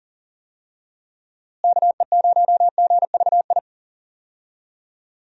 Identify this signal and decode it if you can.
Morse code, heard as a keyed tone, about 40 wpm, 700 Hz. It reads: KE0GVI
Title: USA Call Signs - 6 Characters - Repeated 2x - 40wpm